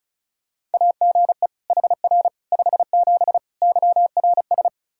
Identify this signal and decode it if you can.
Morse code, heard as a keyed tone, about 36 wpm, 700 Hz. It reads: AGE HR 57 YRS
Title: K1USN SST QSO Practice 35wpm